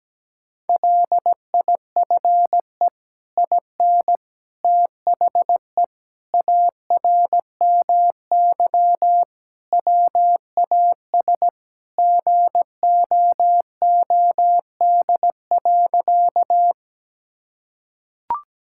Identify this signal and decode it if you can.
Morse code, heard as a keyed tone, about 17 wpm, 700 Hz. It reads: LIFE IN THE ARMY WAS GOOD.
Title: Sentences from Top 1000 Words - Review Part 5 - 17wpm